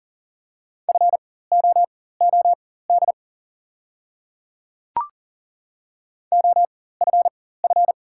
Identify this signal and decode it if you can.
Morse code, heard as a keyed tone, about 39 wpm, spaced as for 18 wpm, 700 Hz. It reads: FOOD OFF
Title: 4x Character Spacing - Top 500 Words - 40wpm